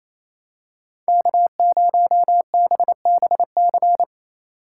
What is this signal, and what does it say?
Morse code, heard as a keyed tone, about 28 wpm, 700 Hz. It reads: K0BBC
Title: USA Call Signs - 5 Characters - 2x Character Spacing - 28wpm